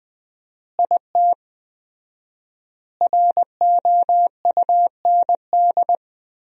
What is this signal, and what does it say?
Morse code, heard as a keyed tone, about 21 wpm, 700 Hz. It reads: IT ROUND
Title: 4x Word-Spacing - Sentences from Top 500 Words - 20wpm